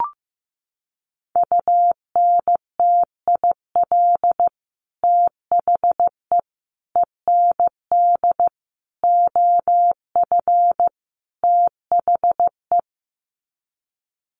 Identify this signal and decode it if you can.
Morse code, heard as a keyed tone, about 15 wpm, 700 Hz. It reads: UNTIL THE END OF THE
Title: Sets of 5 Words Encore 15wpm